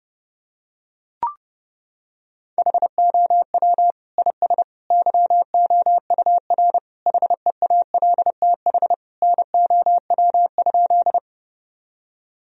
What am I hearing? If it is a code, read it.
Morse code, heard as a keyed tone, about 30 wpm, 700 Hz. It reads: HOW IS YOUR HEALTH NOW?
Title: Sentences from Top 1000 Words - Review Part 3 - 30wpm